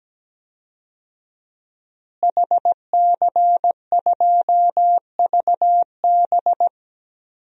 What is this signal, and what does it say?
Morse code, heard as a keyed tone, about 17 wpm, 700 Hz. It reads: HC2VB
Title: DXCC-100 Call Signs - 5 Characters - Repeated 2x - 17wpm